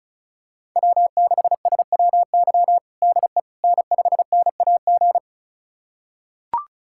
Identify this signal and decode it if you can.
Morse code, heard as a keyed tone, about 36 wpm, 700 Hz. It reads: W6SWY DE N5NAG
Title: USA Callsign DE Callsign 35wpm